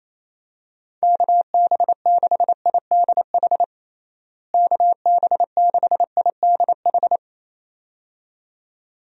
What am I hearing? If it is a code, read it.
Morse code, heard as a keyed tone, about 28 wpm, 700 Hz. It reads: KB6IDH KB6IDH
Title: USA Call Signs - 6 Characters - Repeated 2x - 28wpm